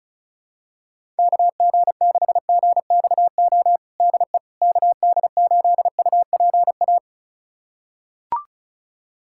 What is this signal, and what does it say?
Morse code, heard as a keyed tone, about 36 wpm, 700 Hz. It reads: KG6GXO DE KD8UPA
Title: USA Callsign DE Callsign - 6 Characters - 35wpm